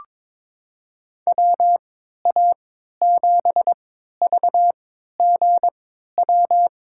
Morse code, heard as a keyed tone, about 23 wpm, spaced as for 13 wpm, 700 Hz. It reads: WA7VGW
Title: USA Call Signs - 6 Characters - 3x Character Spacing - 22wpm